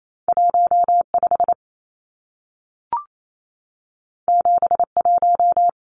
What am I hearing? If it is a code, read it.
Morse code, heard as a keyed tone, about 28 wpm, 700 Hz. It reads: 15 71
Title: Numbers - 2-Digit 28wpm